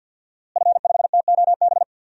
Morse code, heard as a keyed tone, about 49 wpm, 700 Hz. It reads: F5TOB
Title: DXCC-100 Call Signs - 5 Characters - Repeated 3x - 50wpm